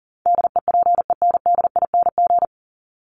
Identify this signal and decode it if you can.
Morse code, heard as a keyed tone, about 40 wpm, 700 Hz. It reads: DEPENDING
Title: ING Suffix - Top 100 Words with Base Words 40wpm